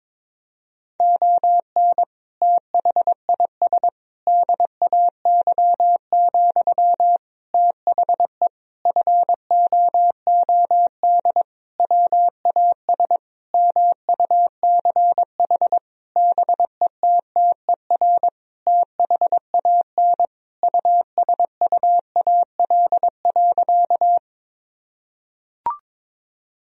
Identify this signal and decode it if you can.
Morse code, heard as a keyed tone, about 22 wpm, 700 Hz. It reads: ON THIS DAY, THE FOOD WAS MUCH BETTER THAN USUAL.
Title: Sentences from Top 1000 Words - Review Part 2 - 22wpm